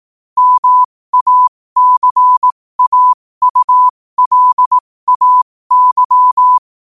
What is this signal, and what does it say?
Morse code, heard as a keyed tone, about 18 wpm, spaced as for 15 wpm, 1000 Hz. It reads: MACAULAY
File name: qotdfiles / 15wpm